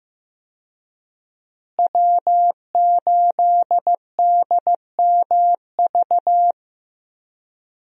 Morse code, heard as a keyed tone, about 15 wpm, 700 Hz. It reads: W8DMV
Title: USA Call Signs - 5 Characters - Repeated 2x - 15wpm